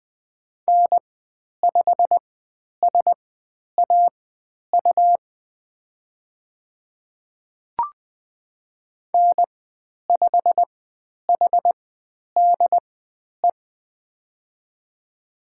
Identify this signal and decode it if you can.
Morse code, heard as a keyed tone, about 20 wpm, spaced as for 10 wpm, 700 Hz. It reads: N5SAU N5HDE
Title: 109 Lesson 18g - Letter m - Call Signs 20wpm